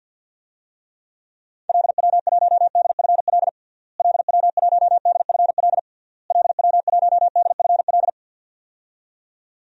Morse code, heard as a keyed tone, about 53 wpm, 700 Hz. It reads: LW1DFL LW1DFL LW1DFL
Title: DXCC-100 Call Signs - 6 Characters - Repeated 3x - 50wpm